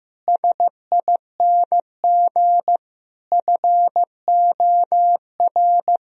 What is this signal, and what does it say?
Morse code, heard as a keyed tone, about 15 wpm, 700 Hz. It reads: SING FOR
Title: Sentences from Top 1000 Words - Review Part 2 - 15wpm